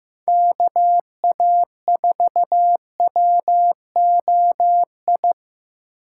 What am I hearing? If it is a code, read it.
Morse code, heard as a keyed tone, about 17 wpm, 700 Hz. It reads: KA4WOI
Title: Speed Racing - USA Call Signs 15wpm